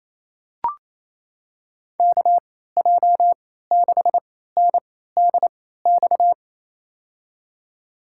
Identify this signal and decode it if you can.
Morse code, heard as a keyed tone, about 27 wpm, spaced as for 16 wpm, 700 Hz. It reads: KJ6NDX
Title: USA Call Signs - 6 Characters - 3x Character Spacing - 28wpm